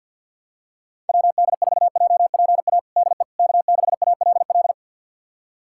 Morse code, heard as a keyed tone, about 51 wpm, 690 Hz. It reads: WD4JPA DE K6ALL
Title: USA Callsign DE Callsign 50wpm